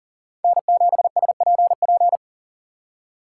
Morse code, heard as a keyed tone, about 37 wpm, 700 Hz. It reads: N7SPP
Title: USA Call Signs - 5 Characters - 3x Character Spacing - 40wpm